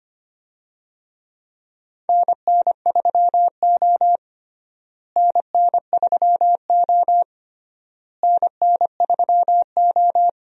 Morse code, heard as a keyed tone, about 25 wpm, 700 Hz. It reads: NN3O NN3O NN3O
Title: USA Call Signs - 4 Characters - Repeated 3x - 25wpm